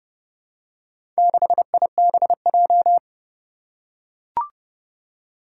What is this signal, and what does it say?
Morse code, heard as a keyed tone, about 30 wpm, 700 Hz. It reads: 6IBJ
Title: Four Challenging Letter-Number Combinations 30wpm